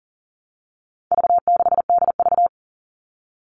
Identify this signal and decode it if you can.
Morse code, heard as a keyed tone, about 40 wpm, 700 Hz. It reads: V6DV
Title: All Cut Numbers - 4-Digit 40wpm